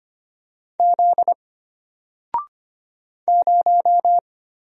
Morse code, heard as a keyed tone, about 25 wpm, 700 Hz. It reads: Z 0
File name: Single-Letter-Number-25wpm